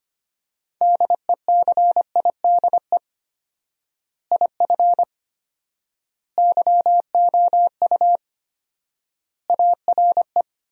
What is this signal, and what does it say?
Morse code, heard as a keyed tone, about 25 wpm, 700 Hz. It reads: DECIDE IF YOU ARE
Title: 4x Word-Spacing - Sentences from Top 500 Words - 25wpm